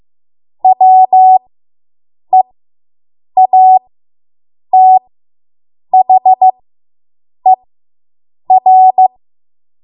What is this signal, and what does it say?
Morse code, heard as a keyed tone, about 15 wpm, spaced as for 7 wpm, 750 Hz. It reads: WEATHER